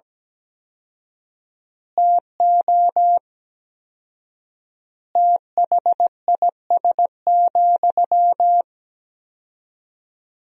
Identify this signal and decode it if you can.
Morse code, heard as a keyed tone, about 17 wpm, 700 Hz. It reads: TO THIS,
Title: 4x Word-Spacing - Sentences from Top 400 Words - 17wpm